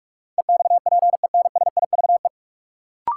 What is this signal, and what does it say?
Morse code, heard as a keyed tone, about 44 wpm, 700 Hz. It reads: EXPENSIVE
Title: Sentences from Top 1500 Words - New Word Drill Alt - 45wpm